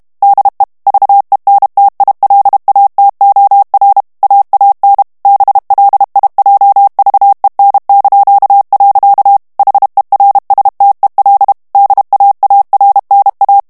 Morse code, heard as a keyed tone, about 32 wpm, 800 Hz. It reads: DE VENTILATOR AAN BLIJVEN). HERSTEL DAARNA
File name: HOLLANDAIS texte clair 2 vitesse 30 mots minute